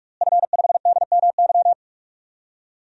Morse code, heard as a keyed tone, about 44 wpm, 700 Hz. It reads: F5DMY
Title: DXCC-100 Call Signs - 5 Characters - Repeated 2x - 45wpm